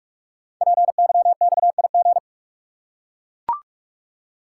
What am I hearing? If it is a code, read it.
Morse code, heard as a keyed tone, about 44 wpm, 690 Hz. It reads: PYXIG
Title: Code Groups - 5 Letters - 45wpm